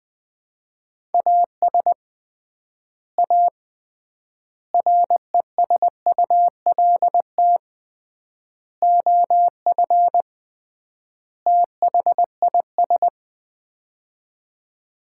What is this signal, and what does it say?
Morse code, heard as a keyed tone, about 21 wpm, 700 Hz. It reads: AS A RESULT OF THIS
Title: 3x Word-Spacing - Sets of 5 Words - 20wpm